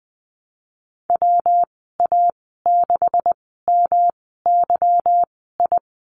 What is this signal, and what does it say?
Morse code, heard as a keyed tone, about 20 wpm, spaced as for 15 wpm, 700 Hz. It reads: WA6MYI
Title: 167 Lesson 27g - Number 7 - Call Signs 20wpm